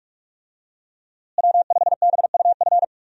Morse code, heard as a keyed tone, about 42 wpm, 700 Hz. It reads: W5BUF
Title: USA Call Signs - 5 Characters - Repeated 3x - 45wpm